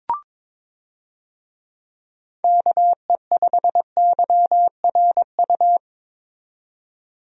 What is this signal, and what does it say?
Morse code, heard as a keyed tone, about 22 wpm, 700 Hz. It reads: KE5YRU
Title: USA Call Signs - 6 Characters - 22wpm